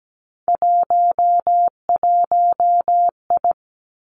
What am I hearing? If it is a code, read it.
Morse code, heard as a keyed tone, about 17 wpm, 700 Hz. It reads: 11I
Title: Three Challenging Letter-Number Combinations 17wpm